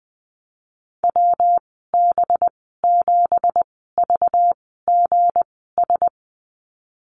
Morse code, heard as a keyed tone, about 19 wpm, spaced as for 14 wpm, 700 Hz. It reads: WB7VGS
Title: 219 Lesson 35g - Number 0 - Call Signs 20wpm